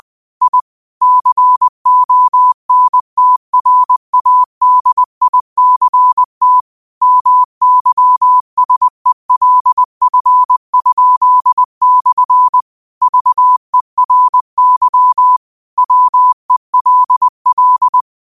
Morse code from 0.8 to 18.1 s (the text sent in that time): CONTRADICT MYSELF?/ VERY WELL